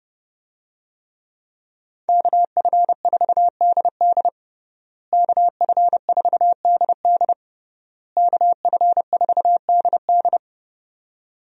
Morse code, heard as a keyed tone, about 30 wpm, 700 Hz. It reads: KF4DD KF4DD KF4DD
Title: USA Call Signs - 5 Characters - Repeated 3x - 30wpm